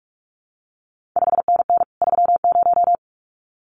Morse code, heard as a keyed tone, about 46 wpm, 700 Hz. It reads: 5NN 30
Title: CQ WW DX Contest Exchange Encore - 45wpm